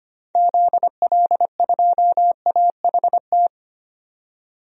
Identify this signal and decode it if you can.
Morse code, heard as a keyed tone, about 25 wpm, 700 Hz. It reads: ZL2AHT
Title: DXCC-100 Call Signs - 6 Characters - 25wpm